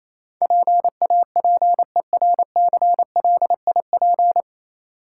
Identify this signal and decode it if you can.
Morse code, heard as a keyed tone, about 27 wpm, 690 Hz. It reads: PAPERCLIP
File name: Easy Words - Speed-Racing with Hints - 22wpm